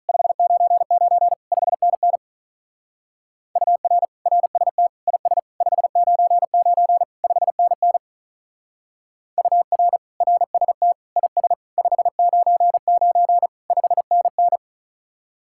Morse code, heard as a keyed tone, about 47 wpm, 700 Hz. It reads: 599 5NN UR RST IS 599 5NN UR RST IS 599 5NN